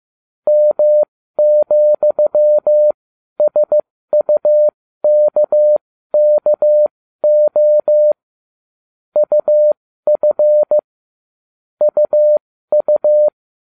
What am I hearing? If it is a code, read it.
Morse code, heard as a keyed tone, about 15 wpm, spaced as for 12 wpm, 600 Hz. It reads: M,SUKKO UF UU
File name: cw / ubacw20 2